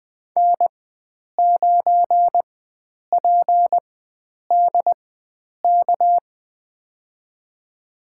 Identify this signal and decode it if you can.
Morse code, heard as a keyed tone, about 21 wpm, spaced as for 9 wpm, 700 Hz. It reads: N9PDK
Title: USA Call Signs - 5 Characters - 4x Character Spacing - 20wpm